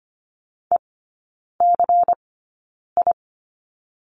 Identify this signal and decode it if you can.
Morse code, heard as a keyed tone, about 25 wpm, spaced as for 9 wpm, 700 Hz. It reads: ECI
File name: course / 25:10wpm / 093 Lesson 16e - Letter c - Three Characters 25wpm